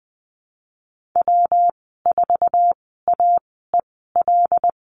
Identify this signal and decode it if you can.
Morse code, heard as a keyed tone, about 20 wpm, spaced as for 14 wpm, 700 Hz. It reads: W4AEL
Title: 219 Lesson 35g - Number 0 - Call Signs 20wpm